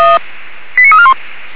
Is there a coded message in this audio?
Yes: five-tone selective call (ZVEI) from 0.8 to 1.1 s, heard as quick runs of five tones, tones of about 70 ms. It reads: 89341